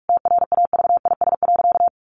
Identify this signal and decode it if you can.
Morse code, heard as a keyed tone, about 44 wpm, 700 Hz. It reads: TRAVIS.
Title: A Sound Of Thunder 45wpm